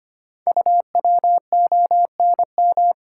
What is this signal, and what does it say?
Morse code, heard as a keyed tone, about 25 wpm, 700 Hz. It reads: UWONM
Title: Code Groups - 5 Letters - 25wpm